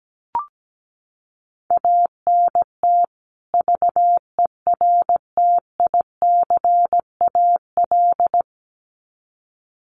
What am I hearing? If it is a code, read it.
Morse code, heard as a keyed tone, about 17 wpm, 700 Hz. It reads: ANT VERTICAL
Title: Common QSO Phrases 17wpm